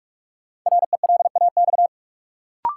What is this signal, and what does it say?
Morse code, heard as a keyed tone, about 42 wpm, 700 Hz. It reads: RELAX
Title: Sentences from Top 1400 Words - New Word Drill - 45wpm